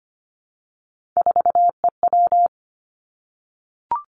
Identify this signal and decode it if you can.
Morse code, heard as a keyed tone, about 25 wpm, 700 Hz. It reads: 4EW
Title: Three Challenging Letter-Number Combinations 25wpm